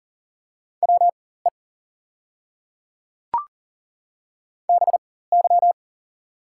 Morse code, heard as a keyed tone, about 42 wpm, spaced as for 19 wpm, 690 Hz. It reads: WE BY
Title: 4x Character Spacing - Top 100 Words - 40wpm